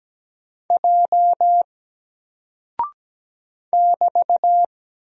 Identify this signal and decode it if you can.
Morse code, heard as a keyed tone, about 17 wpm, 700 Hz. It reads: J <BT>
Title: Single Character (Letter:Number:Punctuation:Prosign) Alt 17wpm